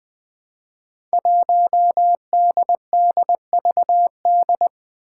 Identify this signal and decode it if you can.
Morse code, heard as a keyed tone, about 21 wpm, 700 Hz. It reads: 1DDVD
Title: Five Challenging Letter-Number Combinations 20wpm